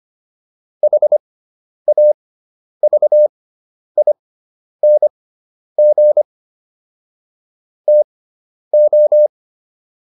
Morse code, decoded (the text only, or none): HAVING TO